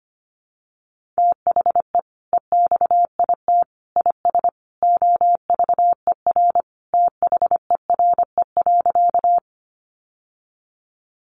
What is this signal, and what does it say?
Morse code, heard as a keyed tone, about 25 wpm, 700 Hz. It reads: THE EXIT IS OVER THERE.
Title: Sentences from Top 1100 Words 25wpm